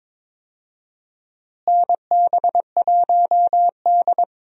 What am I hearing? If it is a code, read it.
Morse code, heard as a keyed tone, about 22 wpm, 700 Hz. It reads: NB1D
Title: USA Call Signs - 4 Characters - Repeated 3x - 22wpm